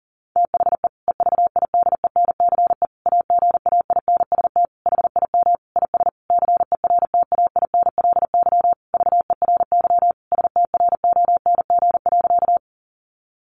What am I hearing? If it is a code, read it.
Morse code, heard as a keyed tone, about 40 wpm, 700 Hz. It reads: THE EVIDENCE AGAINST HIM IS CERTAINLY VERY STRONG.
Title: Sentences from Top 1300 Words 40wpm